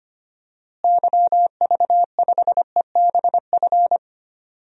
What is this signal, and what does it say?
Morse code, heard as a keyed tone, about 25 wpm, 700 Hz. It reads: YV5EBF
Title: DXCC-100 Call Signs - 6 Characters - Repeated 2x - 25wpm